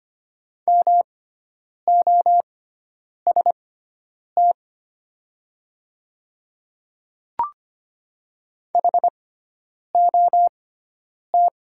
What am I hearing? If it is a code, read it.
Morse code, heard as a keyed tone, about 25 wpm, spaced as for 9 wpm, 700 Hz. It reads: MOST HOT
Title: 6x Character Spacing - Top 200 Words - 25wpm